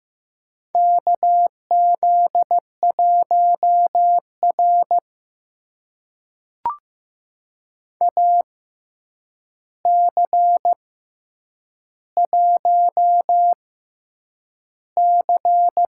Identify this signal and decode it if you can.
Morse code, heard as a keyed tone, about 15 wpm, spaced as for 8 wpm, 700 Hz. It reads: KZ1R AC1C
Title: USA Call Signs - 4 Characters - 6x Character Spacing - 15wpm